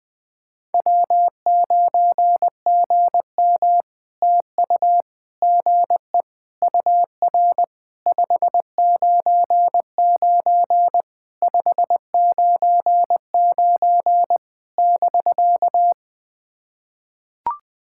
Morse code, heard as a keyed tone, about 20 wpm, 700 Hz. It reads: W9GM TU GE UR 599 599 <BK>
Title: POTA QSO Practice - 20wpm